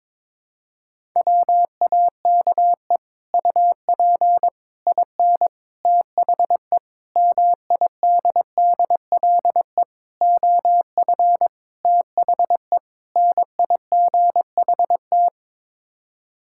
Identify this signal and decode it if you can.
Morse code, heard as a keyed tone, about 22 wpm, 700 Hz. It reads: WAKE UP IN THE MIDDLE OF THE NIGHT
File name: practice / 22wpm / Sets of 8 Words - 22wpm